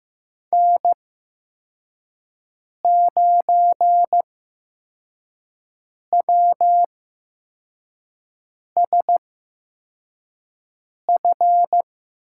Morse code, heard as a keyed tone, about 15 wpm, spaced as for 4 wpm, 700 Hz. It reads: N9WSF
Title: USA Call Signs - 8x Character Spacing - 15wpm